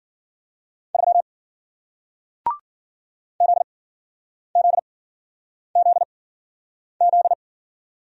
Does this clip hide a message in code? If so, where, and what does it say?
Yes: Morse code from 0.9 to 7.3 s, heard as a keyed tone, about 53 wpm, spaced as for 9 wpm, 700 Hz. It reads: 3 ZZZZ